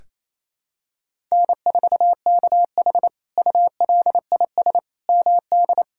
Morse code, heard as a keyed tone, about 28 wpm, 700 Hz. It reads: N4KH ULIS MD
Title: K1USN SST QSO Practice - 28wpm